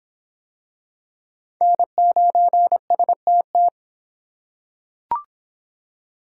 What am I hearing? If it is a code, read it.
Morse code, heard as a keyed tone, about 25 wpm, 700 Hz. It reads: N9STT